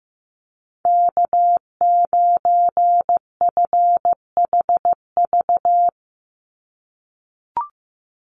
Morse code, heard as a keyed tone, about 15 wpm, 700 Hz. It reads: K9FHV
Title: USA Call Signs 15wpm